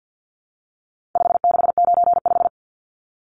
Morse code, heard as a keyed tone, about 50 wpm, 700 Hz. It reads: 5695
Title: Numbers - 4-Digit 50wpm